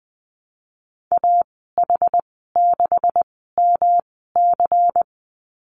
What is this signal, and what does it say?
Morse code, heard as a keyed tone, about 20 wpm, spaced as for 14 wpm, 700 Hz. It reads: AH6MC
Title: 154 Lesson 25g - Letter p - Call Signs 20wpm